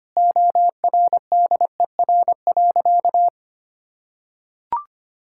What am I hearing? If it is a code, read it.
Morse code, heard as a keyed tone, about 25 wpm, 700 Hz. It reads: ORDER.
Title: Sentences from Top 1000 Words - Review Part 6 - 25wpm